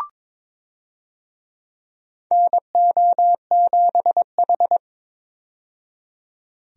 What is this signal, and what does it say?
Morse code, heard as a keyed tone, about 22 wpm, 700 Hz. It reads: NO7H
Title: USA Call Signs - 4 Characters - 22wpm